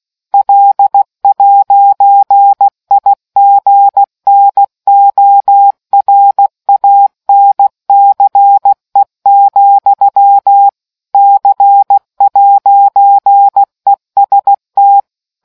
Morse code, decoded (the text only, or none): L'IGNORANCE, C'EST